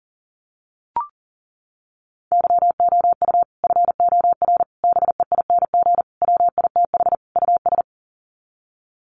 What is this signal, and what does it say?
Morse code, heard as a keyed tone, about 40 wpm, 700 Hz. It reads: YOU FOR BEING WITH US
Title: Sets of 5 Words 40wpm